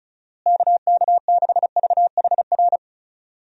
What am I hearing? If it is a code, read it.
Morse code, heard as a keyed tone, about 36 wpm, 700 Hz. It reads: KK6VHR
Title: USA Call Signs - 6 Characters - Repeated 3x - 35wpm